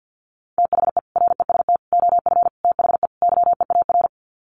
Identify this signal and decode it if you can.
Morse code, heard as a keyed tone, about 50 wpm, 700 Hz. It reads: THE REST OF THE YEAR
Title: Sets of 5 Words Encore 50wpm